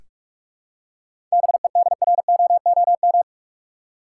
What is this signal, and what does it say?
Morse code, heard as a keyed tone, about 44 wpm, 700 Hz. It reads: BEDROOM
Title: Seven Letter Words - 45wpm